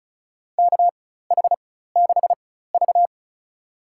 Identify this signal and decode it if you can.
Morse code, heard as a keyed tone, about 34 wpm, spaced as for 16 wpm, 700 Hz. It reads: KH6V
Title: USA Call Signs - 4 Characters - 4x Character Spacing - 35wpm